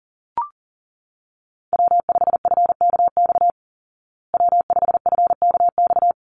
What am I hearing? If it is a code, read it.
Morse code, heard as a keyed tone, about 40 wpm, 700 Hz. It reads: W5FKX W5FKX
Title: Call Signs Repeated 3x - 40wpm